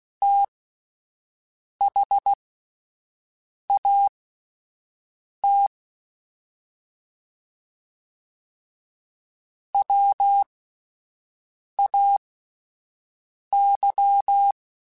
Morse code, heard as a keyed tone, about 16 wpm, spaced as for 5 wpm, 800 Hz. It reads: THAT WAY
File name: (spaced) / QSOs Spaced-mp3s / WA7QOX 1 @10 sp